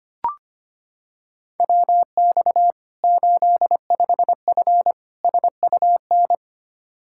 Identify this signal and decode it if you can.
Morse code, heard as a keyed tone, about 25 wpm, 700 Hz. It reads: WX 85F SUN
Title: Practice QSO - Exchange 2 - Info - 25wpm